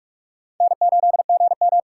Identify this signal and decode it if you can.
Morse code, heard as a keyed tone, about 60 wpm, spaced as for 53 wpm, 700 Hz. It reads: N8GM
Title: Speed Racing - USA 1x2 Call Signs 45wpm